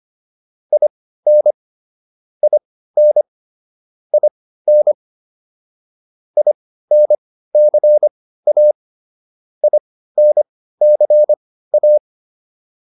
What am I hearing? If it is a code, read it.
Morse code, heard as a keyed tone, about 25 wpm, spaced as for 15 wpm, 600 Hz. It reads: IN IN IN INCA INCA